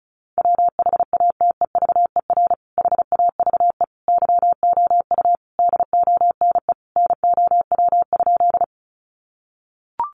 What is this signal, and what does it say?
Morse code, heard as a keyed tone, about 34 wpm, 700 Hz. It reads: WHATEVER HAVE YOU DONE NOW?
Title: Sentences from Top 1100 Words 35wpm